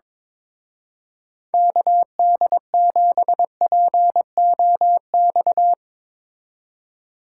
Morse code, heard as a keyed tone, about 22 wpm, 700 Hz. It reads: KD7POX
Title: Speed Racing - USA 2x3 Call Signs 22wpm